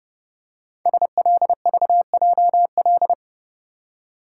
Morse code, heard as a keyed tone, about 30 wpm, 700 Hz. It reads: SLVJL
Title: Code Groups - 5 Letters - 30wpm